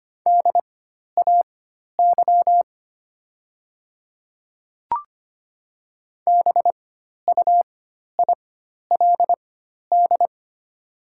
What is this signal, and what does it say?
Morse code, heard as a keyed tone, about 25 wpm, spaced as for 12 wpm, 700 Hz. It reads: DAY BUILD
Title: 4x Character Spacing - Top 200 Words - 25wpm